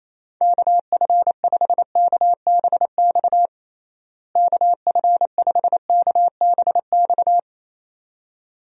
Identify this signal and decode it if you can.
Morse code, heard as a keyed tone, about 28 wpm, 700 Hz. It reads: KF5KBX KF5KBX
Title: USA Call Signs - 6 Characters - Repeated 2x - 28wpm